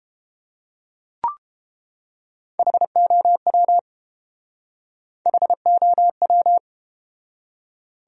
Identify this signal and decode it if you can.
Morse code, heard as a keyed tone, about 31 wpm, 700 Hz. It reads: HOW HOW